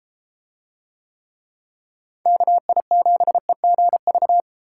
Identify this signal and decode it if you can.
Morse code, heard as a keyed tone, about 32 wpm, 700 Hz. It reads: KI7EGV